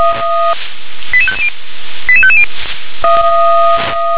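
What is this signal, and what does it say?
Five-tone selective call (ZVEI), heard as quick runs of five tones, tones of about 70 ms. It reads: 88550 88550
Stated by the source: Five-tone ZVEI selective call received on an SDR.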